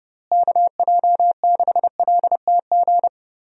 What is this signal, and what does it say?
Morse code, heard as a keyed tone, about 30 wpm, 700 Hz. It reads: KJ6LTG
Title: USA Call Signs - 6 Characters - Repeated 2x - 30wpm